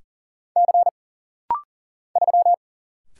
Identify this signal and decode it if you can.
Morse code, heard as a keyed tone, about 40 wpm, 700 Hz. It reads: C 3
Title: Single Letter-Number - Rapid-Fire 40wpm